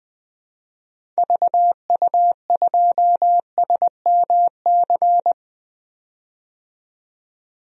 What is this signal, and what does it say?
Morse code, heard as a keyed tone, about 20 wpm, 700 Hz. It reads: VU2SMC
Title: DXCC-100 Call Signs - 6 Characters - 20wpm